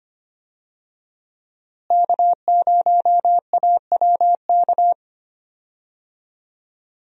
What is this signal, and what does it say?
Morse code, heard as a keyed tone, about 25 wpm, 700 Hz. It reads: K0AWK
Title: USA Call Signs - 5 Characters - Repeated 3x - 25wpm